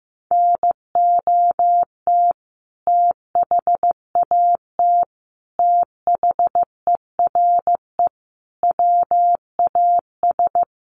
Morse code, heard as a keyed tone, about 15 wpm, 700 Hz. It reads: NOT THAT THERE WAS
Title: Sentences from Top 1400 Words 15wpm